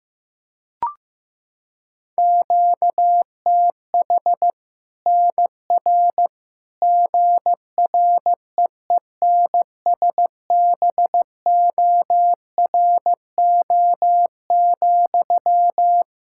Morse code, heard as a keyed tone, about 15 wpm, 700 Hz. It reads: QTH NR GREENSBORO,
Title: Practice QSO - Exchange 1 - Beginning - 15wpm